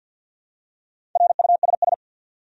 Morse code, heard as a keyed tone, about 47 wpm, 700 Hz. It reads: RUSS
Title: USA Common Names - 50wpm